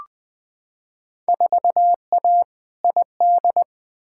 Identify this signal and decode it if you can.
Morse code, heard as a keyed tone, about 20 wpm, 700 Hz. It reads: 4A ID
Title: Field Day Exchanges - 20wpm